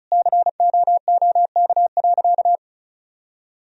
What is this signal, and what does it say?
Morse code, heard as a keyed tone, about 36 wpm, 700 Hz. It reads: COOK.
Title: Sentences from Top 1000 Words - Review Part 4 - 35wpm